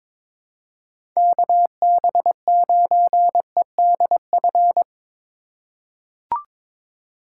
Morse code, heard as a keyed tone, about 22 wpm, 700 Hz. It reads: KB9EDF
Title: USA Call Signs - 6 Characters - 4x Character Spacing - 22wpm